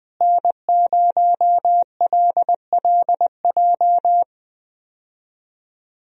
Morse code, heard as a keyed tone, about 20 wpm, 700 Hz. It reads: N0LLJ
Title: USA Call Signs - 5 Characters - Repeated 2x - 20wpm